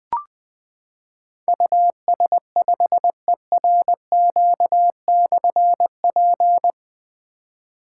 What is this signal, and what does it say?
Morse code, heard as a keyed tone, about 20 wpm, 700 Hz. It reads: US5ERQ/P
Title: Difficult Call Signs 20wpm